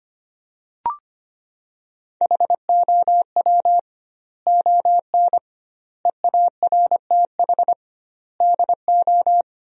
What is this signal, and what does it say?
Morse code, heard as a keyed tone, about 25 wpm, 700 Hz. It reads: HOW ON EARTH DO
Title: 2x Word-Spacing - Sentences from Top 200 Words - 25wpm